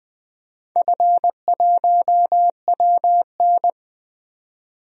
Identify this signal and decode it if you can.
Morse code, heard as a keyed tone, about 20 wpm, 700 Hz. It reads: F1WN
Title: DXCC-100 Call Signs - 4 Characters - 20wpm